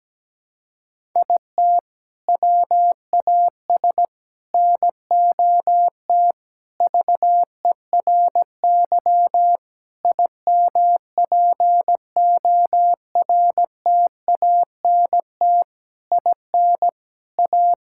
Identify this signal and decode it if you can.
Morse code, heard as a keyed tone, about 17 wpm, 700 Hz. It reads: IT WAS NOT VERY IMPORTANT IN A
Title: Sentences from Top 1000 Words - Review Part 5 - 17wpm